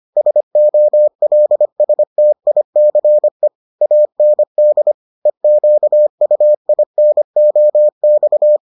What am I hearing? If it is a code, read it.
Morse code, heard as a keyed tone, about 25 wpm, 600 Hz. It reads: SOLSTICE AND EQUINOX